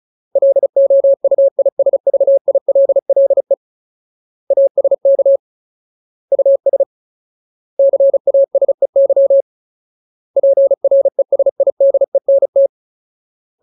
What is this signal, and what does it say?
Morse code, heard as a keyed tone, about 34 wpm, 550 Hz. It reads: LOUISVILLE ASK US CASEY PRESIDENT
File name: Ultimate Word List / 35WPM Ultimate